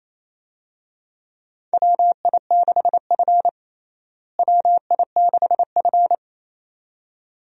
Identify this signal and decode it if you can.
Morse code, heard as a keyed tone, about 28 wpm, 700 Hz. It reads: WI6F WI6F
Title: USA Call Signs - 4 Characters - Repeated 2x - 28wpm